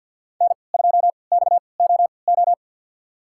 Morse code, heard as a keyed tone, about 48 wpm, spaced as for 28 wpm, 690 Hz. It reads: N2XOO
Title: USA Call Signs - 5 Characters - 3x Character Spacing - 50wpm